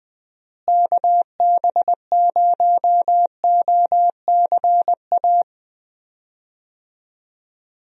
Morse code, heard as a keyed tone, about 20 wpm, 700 Hz. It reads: KB0OCA
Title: USA Call Signs - 6 Characters - Repeated 2x - 20wpm